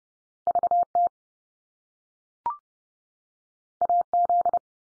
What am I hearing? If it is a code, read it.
Morse code, heard as a keyed tone, about 30 wpm, 700 Hz. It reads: VT AZ
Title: US States 30WPM